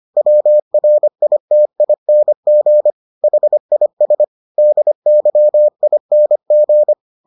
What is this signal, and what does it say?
Morse code, heard as a keyed tone, about 25 wpm, 600 Hz. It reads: WRITING HIS DYING